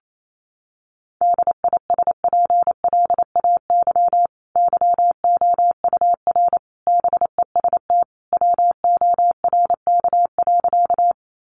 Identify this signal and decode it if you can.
Morse code, heard as a keyed tone, about 28 wpm, 700 Hz. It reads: DISPLAY YOUR BEST WORK.
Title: Sentences from Top 1200 Words 28wpm